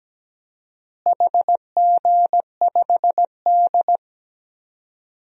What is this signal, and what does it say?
Morse code, heard as a keyed tone, about 17 wpm, 700 Hz. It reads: HG5D
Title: DXCC-100 Call Signs - 4 Characters - Repeated 2x - 17wpm